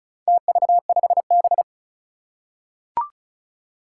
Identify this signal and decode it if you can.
Morse code, heard as a keyed tone, about 36 wpm, 700 Hz. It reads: TV5B
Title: Four Challenging Letter-Number Combinations 35wpm